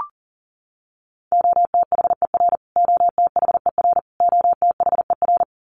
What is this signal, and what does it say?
Morse code, heard as a keyed tone, about 40 wpm, 700 Hz. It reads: OTHER OTHER OTHER
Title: Top 500 Words Repeated 40wpm